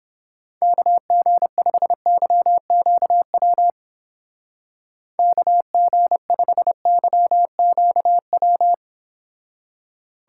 Morse code, heard as a keyed tone, about 29 wpm, 700 Hz. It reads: KG5YQW KG5YQW